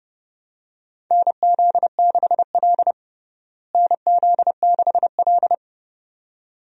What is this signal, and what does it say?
Morse code, heard as a keyed tone, about 32 wpm, 700 Hz. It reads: NZ6L NZ6L
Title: USA Call Signs - 4 Characters - Repeated 2x - 30wpm